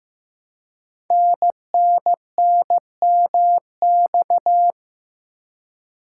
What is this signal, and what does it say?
Morse code, heard as a keyed tone, about 15 wpm, 700 Hz. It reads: NNNMX
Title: Code Groups - 5 Letters - 15wpm